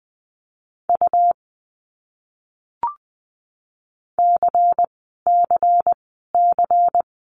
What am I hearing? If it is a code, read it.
Morse code, heard as a keyed tone, about 20 wpm, 700 Hz. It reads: U C C C
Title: Abbreviations Repeated 3x - 20wpm